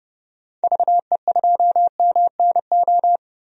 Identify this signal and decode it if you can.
Morse code, heard as a keyed tone, about 30 wpm, 700 Hz. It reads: VE2MNO
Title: DXCC-100 Call Signs - 6 Characters - 30wpm